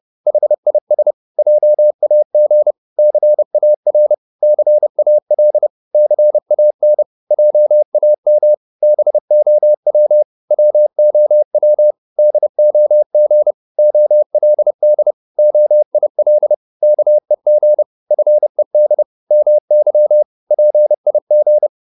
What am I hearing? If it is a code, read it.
Morse code, heard as a keyed tone, about 30 wpm, 600 Hz. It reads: HIS JAG CAR CAL CAN JAM BOW WOW DOG OLD OIL KEG FED MY PIG